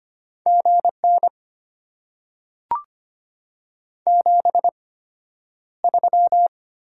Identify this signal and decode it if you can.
Morse code, heard as a keyed tone, about 25 wpm, spaced as for 11 wpm, 700 Hz. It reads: GN 73
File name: Top 90 QSO Elements - 8x Character Spacing - 25wpm